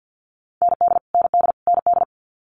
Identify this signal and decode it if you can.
Morse code, heard as a keyed tone, about 50 wpm, 690 Hz. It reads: ND ND ND
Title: Multi-letter Phonemes Repeated 3x 50wpm